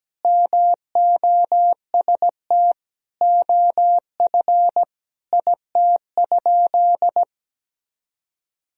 Morse code, heard as a keyed tone, about 17 wpm, 700 Hz. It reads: MOST OF IT?
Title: Sentences from Top 1000 Words - Review Part 1 - 17wpm